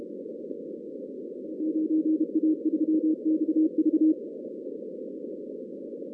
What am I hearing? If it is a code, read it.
Morse code, heard as a keyed tone, about 30 wpm, 310 Hz. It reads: 9A3XV
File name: 2016 arrl dx cw / 2016-ARRL-DX-CW-K5TR- - 0-46